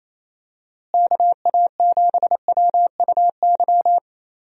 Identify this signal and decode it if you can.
Morse code, heard as a keyed tone, about 27 wpm, 700 Hz. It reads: KA7WUY
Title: USA Call Signs - 6 Characters - Repeated 2x - 28wpm